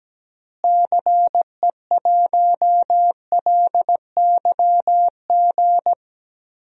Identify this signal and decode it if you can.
Morse code, heard as a keyed tone, about 17 wpm, 700 Hz. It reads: CE1LYG
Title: DXCC-100 Call Signs - 6 Characters - Repeated 3x - 17wpm